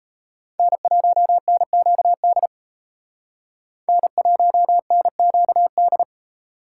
Morse code, heard as a keyed tone, about 34 wpm, 700 Hz. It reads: N1NQD N1NQD
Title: Speed Racing - USA 1x3 Call Signs 28wpm